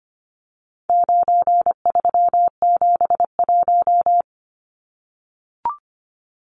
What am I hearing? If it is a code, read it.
Morse code, heard as a keyed tone, about 25 wpm, 700 Hz. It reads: 9371
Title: Numbers - 4-Digit 25wpm